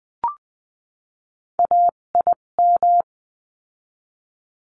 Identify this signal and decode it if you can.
Morse code, heard as a keyed tone, about 20 wpm, spaced as for 17 wpm, 700 Hz. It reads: AIM
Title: 257 Lesson 42b - Words 14-20wpm